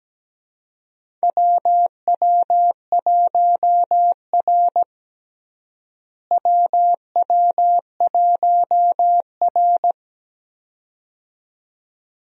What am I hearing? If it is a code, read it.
Morse code, heard as a keyed tone, about 17 wpm, 700 Hz. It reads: WW1R WW1R
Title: USA Call Signs - 4 Characters - Repeated 2x - 17wpm